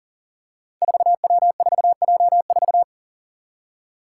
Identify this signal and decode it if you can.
Morse code, heard as a keyed tone, about 40 wpm, 700 Hz. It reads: 4W4J4
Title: Five Challenging Letter-Number Combinations 40wpm